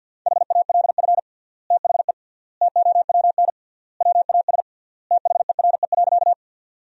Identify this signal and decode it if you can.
Morse code, heard as a keyed tone, about 53 wpm, 700 Hz. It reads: HALF THE TOWN WAS THERE.
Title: 3x Word-Spacing - Sentences from Top 500 Words - 50wpm